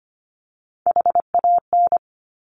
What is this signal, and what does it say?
Morse code, heard as a keyed tone, about 25 wpm, 700 Hz. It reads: HAN
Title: Top Three Letter Combinations in Words 25wpm